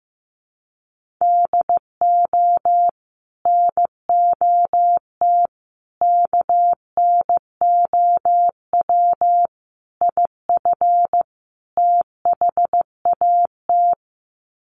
Morse code, heard as a keyed tone, about 15 wpm, 700 Hz. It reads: DO NOT KNOW IF THAT
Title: Sets of 5 Words Encore2 15wpm